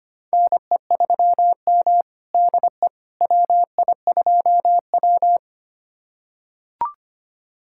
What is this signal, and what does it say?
Morse code, heard as a keyed tone, about 25 wpm, 700 Hz. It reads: NE3M DE WI2W
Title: USA Callsign DE Callsign - 4 Characters - 25wpm